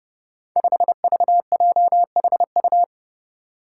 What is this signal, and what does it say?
Morse code, heard as a keyed tone, about 30 wpm, 700 Hz. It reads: 5VJHU
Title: Five Challenging Letter-Number Combinations 30wpm